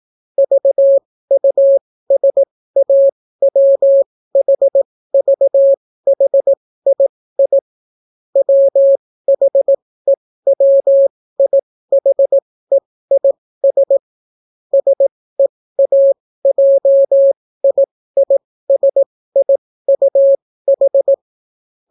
Morse code, heard as a keyed tone, about 18 wpm, spaced as for 15 wpm, 550 Hz. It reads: VUSAWHVHII WHEWIHEIS SEAJIISIUH